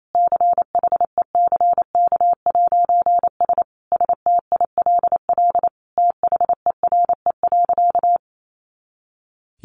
Morse code, heard as a keyed tone, about 28 wpm, 700 Hz. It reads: CHECK'S STILL THERE.
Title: A Sound Of Thunder 28wpm